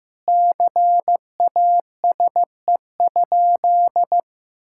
Morse code, heard as a keyed tone, about 15 wpm, 700 Hz. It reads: CASE?
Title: Sentences from Top 1000 Words - Review Part 5 - 15wpm